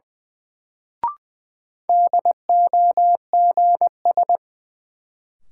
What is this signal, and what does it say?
Morse code, heard as a keyed tone, about 20 wpm, 700 Hz. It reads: DOGS
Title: Sentences from Top 1600 Words - New Word Drill Alt - 20wpm